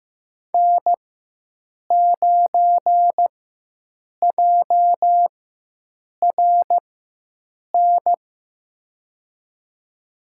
Morse code, heard as a keyed tone, about 15 wpm, spaced as for 7 wpm, 700 Hz. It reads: N9JRN
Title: USA Call Signs - 5 Characters - 4x Character Spacing - 15wpm